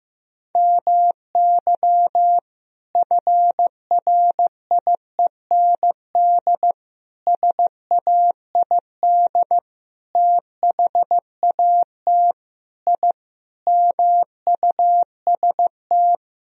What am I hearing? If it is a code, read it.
Morse code, heard as a keyed tone, about 16 wpm, 700 Hz. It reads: MY FRIEND SAID THAT I MUST
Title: Sentences from Top 1000 Words - Review Part 4 - 15wpm